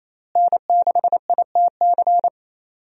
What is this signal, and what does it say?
Morse code, heard as a keyed tone, about 28 wpm, 700 Hz. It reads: N6ITC
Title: USA Call Signs - 5 Characters - Repeated 2x - 28wpm